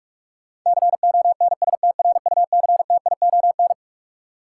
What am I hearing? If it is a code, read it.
Morse code, heard as a keyed tone, about 46 wpm, 700 Hz. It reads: CONSTRUCTION
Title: Sentences from Top 1600 Words - New Word Drill - 45wpm